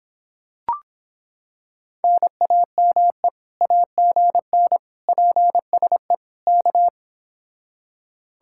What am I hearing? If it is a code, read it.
Morse code, heard as a keyed tone, about 26 wpm, 700 Hz. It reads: NAME AGN PSE K